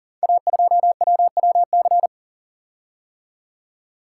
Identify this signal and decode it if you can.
Morse code, heard as a keyed tone, about 40 wpm, 690 Hz. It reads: A2WWC
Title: Code Groups - 5 Letter:Numbers - 40wpm